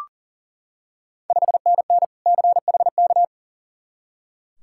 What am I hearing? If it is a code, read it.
Morse code, heard as a keyed tone, about 40 wpm, 690 Hz. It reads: 5NN CHK
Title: Ontario QSO Party Exchange 40wpm